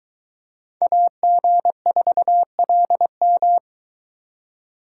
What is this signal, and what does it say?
Morse code, heard as a keyed tone, about 23 wpm, 700 Hz. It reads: AG4LM